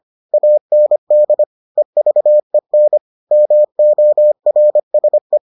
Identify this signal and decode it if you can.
Morse code, heard as a keyed tone, about 25 wpm, 600 Hz. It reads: AND EVEN MORSE